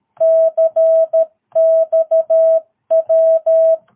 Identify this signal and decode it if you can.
Morse code, heard as a keyed tone, about 13 wpm, 650 Hz. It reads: CXW
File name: jh8cbh / 2022cw